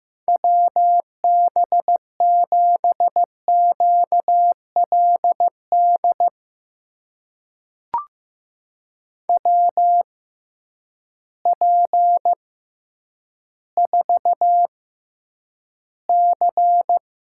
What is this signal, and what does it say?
Morse code, heard as a keyed tone, about 15 wpm, spaced as for 9 wpm, 700 Hz. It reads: WB7QLD WP4C
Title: USA Call Signs - 6 Characters - 6x Character Spacing - 15wpm